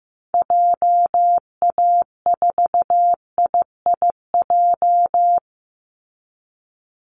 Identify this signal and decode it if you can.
Morse code, heard as a keyed tone, about 15 wpm, 700 Hz. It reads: JA4IIJ
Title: Call Signs Repeated 3x - 15wpm